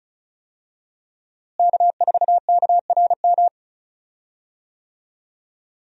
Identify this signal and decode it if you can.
Morse code, heard as a keyed tone, about 36 wpm, 700 Hz. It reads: K4KRM
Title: USA Call Signs - 5 Characters - Repeated 3x - 35wpm